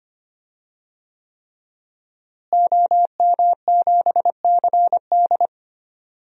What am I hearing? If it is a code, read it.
Morse code, heard as a keyed tone, about 25 wpm, 700 Hz. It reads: OM7CD
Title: DXCC-100 Call Signs - 5 Characters - Repeated 3x - 25wpm